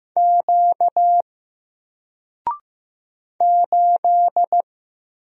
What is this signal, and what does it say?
Morse code, heard as a keyed tone, about 15 wpm, 700 Hz. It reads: Q 8
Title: Single Character (Letter:Number:Punctuation:Prosign) 15wpm